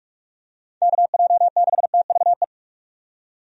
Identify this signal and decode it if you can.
Morse code, heard as a keyed tone, about 42 wpm, 700 Hz. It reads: KJ6TVE
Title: USA Call Signs - 6 Characters - 45wpm